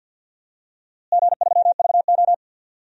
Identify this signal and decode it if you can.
Morse code, heard as a keyed tone, about 53 wpm, 700 Hz. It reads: G3VO
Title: DXCC-100 Call Signs - 4 Characters - Repeated 2x - 50wpm